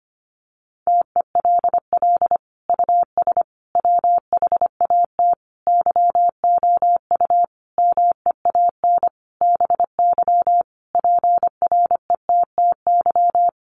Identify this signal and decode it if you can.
Morse code, heard as a keyed tone, about 25 wpm, 700 Hz. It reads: TELL US WHAT YOU MEAN BY PRETTY
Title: Sentences from Top 1000 Words 25wpm